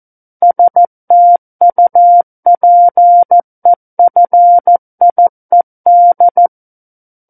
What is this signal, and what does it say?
Morse code, heard as a keyed tone, about 14 wpm, 700 Hz. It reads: STUPEFIED
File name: Book One / 05-Chapter 5 - 14 WPM